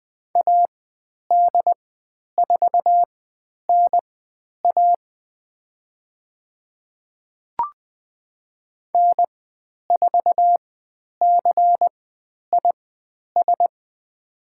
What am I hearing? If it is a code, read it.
Morse code, heard as a keyed tone, about 20 wpm, spaced as for 10 wpm, 700 Hz. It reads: AD4NA N4CIS
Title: 102 Lesson 17g - Letter u - Call Signs 20wpm